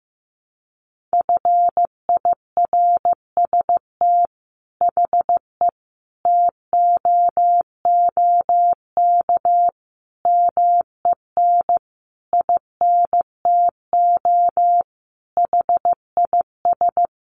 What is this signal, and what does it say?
Morse code, heard as a keyed tone, about 15 wpm, 700 Hz. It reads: FIRST HE TOOK MEN INTO HIS
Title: Sentences from Top 1500 Words 15wpm